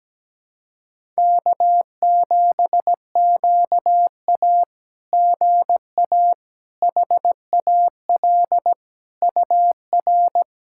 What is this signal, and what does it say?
Morse code, heard as a keyed tone, about 17 wpm, 700 Hz. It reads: K7QA GA HAL UR
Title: SOTA QSO Practice 17wpm